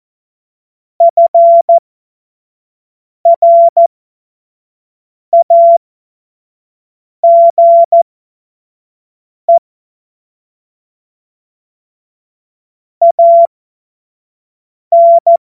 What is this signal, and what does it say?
Morse code, heard as a keyed tone, about 14 wpm, spaced as for 5 wpm, 680 Hz. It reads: FRAGE AN
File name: Morsekurs / Witze / Witz 25. Juli 2023 25 F70